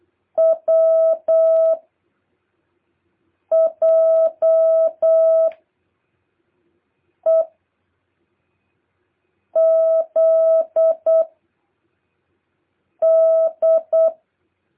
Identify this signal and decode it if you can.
Morse code, heard as a keyed tone, about 8 wpm, 650 Hz. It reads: W J E Z D
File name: jh8cbh / 2023cw